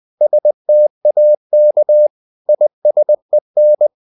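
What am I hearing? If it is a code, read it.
Morse code, heard as a keyed tone, about 20 wpm, 600 Hz. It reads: STAK ISEN
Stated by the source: Morse code practice of short Danish words at 20 wpm, from CW med Gnister.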